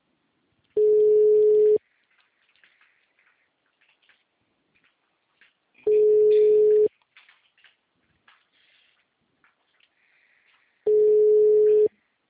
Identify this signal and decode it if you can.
Ringback tone (Europe, 425 Hz).
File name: out- -2009- - - .41600